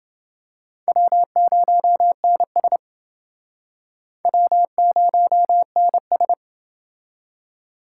Morse code, heard as a keyed tone, about 30 wpm, 700 Hz. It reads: W0NS W0NS